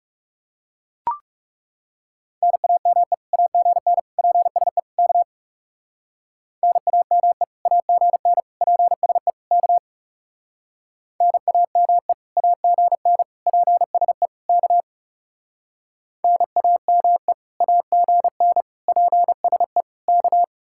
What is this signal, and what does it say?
Morse code, heard as a keyed tone, about 40 wpm, 700 Hz. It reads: NAME AGN PSE K NAME AGN PSE K NAME AGN PSE K NAME AGN PSE K